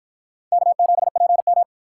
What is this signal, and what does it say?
Morse code, heard as a keyed tone, about 53 wpm, 700 Hz. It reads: K7PM